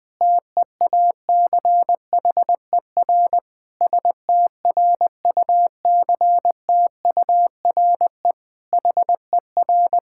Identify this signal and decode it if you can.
Morse code, heard as a keyed tone, about 20 wpm, 700 Hz. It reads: TEACHER STRUCTURE HER
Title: Sentences from Top 1000 Words - Review Part 6 - 20wpm